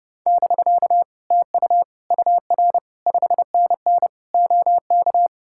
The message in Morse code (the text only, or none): <BK> TU UR 5NN OK